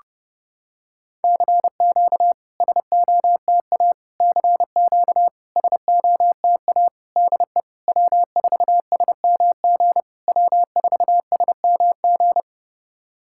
Morse code, heard as a keyed tone, about 30 wpm, 700 Hz. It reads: CQ SOTA CQ SOTA DE W4SMG W4SMG
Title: SOTA QSO Practice 30wpm